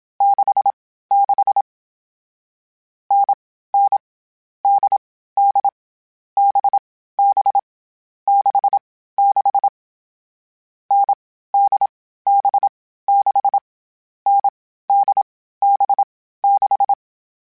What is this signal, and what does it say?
Morse code, heard as a keyed tone, about 27 wpm, 800 Hz. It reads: 6 6 N N D D B B 6 6 N D B 6 N D B 6
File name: Short Runs / Short Runs-mp3s / NDB NDB6 @25